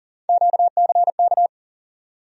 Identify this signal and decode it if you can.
Morse code, heard as a keyed tone, about 40 wpm, 700 Hz. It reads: QCK
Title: USA Call Sign Postfixes - 3 Letters 40wpm